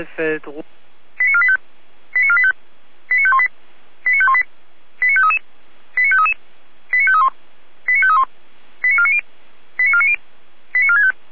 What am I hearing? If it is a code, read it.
Five-tone selective call (ZVEI), heard as quick runs of five tones, tones of about 70 ms, with speech. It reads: 89485 89485 89518 89518 89533 89533 89531 89531 89490 89490 89465